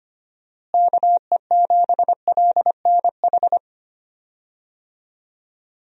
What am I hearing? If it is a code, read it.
Morse code, heard as a keyed tone, about 25 wpm, 700 Hz. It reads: KE7LNH
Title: USA Call Signs - 6 Characters - Repeated 3x - 25wpm